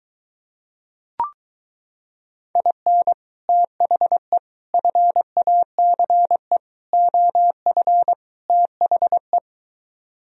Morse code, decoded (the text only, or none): IN THE FACE OF THE